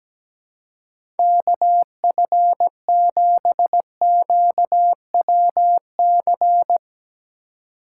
Morse code, heard as a keyed tone, about 17 wpm, 700 Hz. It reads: KF7QWC
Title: USA Call Signs - 6x Character Spacing - 17wpm